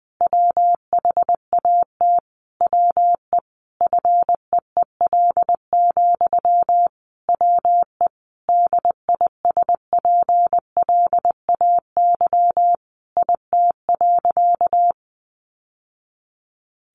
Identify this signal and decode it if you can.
Morse code, heard as a keyed tone, about 20 wpm, 700 Hz. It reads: WHAT WE FEEL, WE DISPLAY IT.
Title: Sentences from Top 1200 Words 20wpm